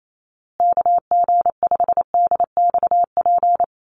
Morse code, heard as a keyed tone, about 28 wpm, 690 Hz. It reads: KG5DXP
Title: USA Call Signs Repeated 3x 28wpm